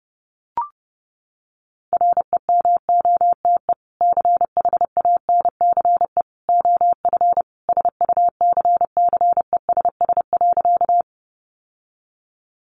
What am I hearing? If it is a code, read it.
Morse code, heard as a keyed tone, about 30 wpm, 700 Hz. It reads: REMOTE CHANCE OF SUCCESS.
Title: Sentences from Top 1300 Words 30wpm